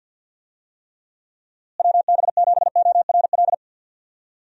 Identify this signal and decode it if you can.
Morse code, heard as a keyed tone, about 51 wpm, 700 Hz. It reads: WB7ORL
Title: USA Call Signs - 6 Characters - Repeated 3x - 50wpm